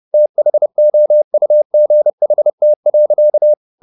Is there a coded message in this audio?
Yes: Morse code from 0.1 to 3.5 s, heard as a keyed tone, about 30 wpm, 600 Hz. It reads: THOUGHT.